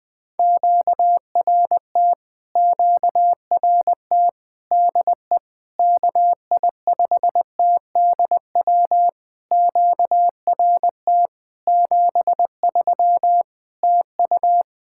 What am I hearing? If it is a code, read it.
Morse code, heard as a keyed tone, about 20 wpm, 700 Hz. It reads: QRT QRT DE KI5TDW QRT 73 TU
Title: POTA QSO Practice - 20wpm